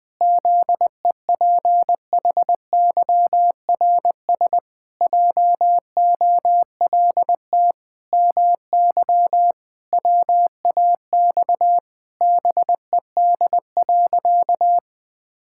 Morse code, heard as a keyed tone, about 19 wpm, 700 Hz. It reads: ZEPHYRS JOLT MY WAX BED.
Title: Pangram Challenge v2 - 20wpm